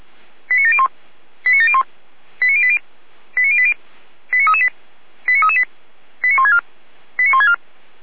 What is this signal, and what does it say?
Five-tone selective call (ZVEI), heard as quick runs of five tones, tones of about 70 ms. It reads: 89701 89701 89080 89080 89338 89338 89164 89164
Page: /455/SDR/ZVEI/